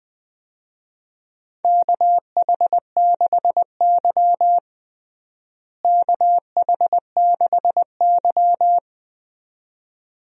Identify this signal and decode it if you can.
Morse code, heard as a keyed tone, about 20 wpm, 700 Hz. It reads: KH6Y KH6Y
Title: DXCC-100 Call Signs - 4 Characters - Repeated 2x - 20wpm